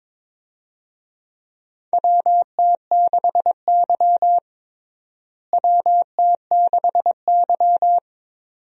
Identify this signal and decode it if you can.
Morse code, heard as a keyed tone, about 22 wpm, 700 Hz. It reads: WT6Y WT6Y
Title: USA Call Signs - 4 Characters - Repeated 2x - 22wpm